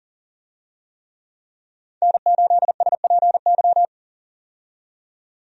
Morse code, heard as a keyed tone, about 40 wpm, 700 Hz. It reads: N8SPY
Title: USA Call Signs - 5 Characters - 40wpm